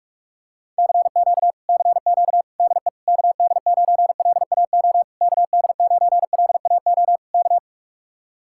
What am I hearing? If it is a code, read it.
Morse code, heard as a keyed tone, about 49 wpm, 700 Hz. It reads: CQ CQ DE KD9LAO KD9LAO K